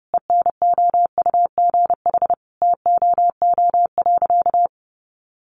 Morse code, heard as a keyed tone, about 28 wpm, 700 Hz. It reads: ENOUGH TOO.
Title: Easy Phrases 30wpm